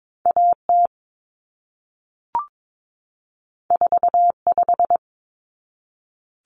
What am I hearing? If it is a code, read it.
Morse code, heard as a keyed tone, about 22 wpm, 700 Hz. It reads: AT 45
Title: Common Cut Numbers - 2-Digit 22wpm